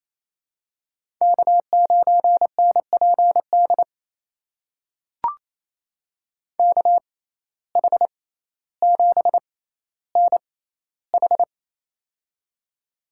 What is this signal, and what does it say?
Morse code, heard as a keyed tone, about 28 wpm, spaced as for 14 wpm, 700 Hz. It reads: K9NPD KH7NH
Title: USA Call Signs - 5 Characters - 6x Character Spacing - 28wpm